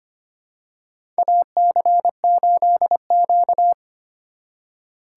Morse code, heard as a keyed tone, about 25 wpm, 700 Hz. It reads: AC8Q
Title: USA Call Signs - 4 Characters - 2x Character Spacing - 25wpm